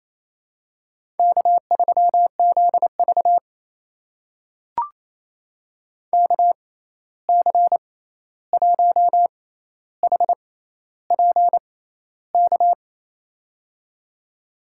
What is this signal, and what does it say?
Morse code, heard as a keyed tone, about 28 wpm, spaced as for 13 wpm, 700 Hz. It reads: K3ZV KC1HPK
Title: USA Call Signs - 6x Character Spacing - 28wpm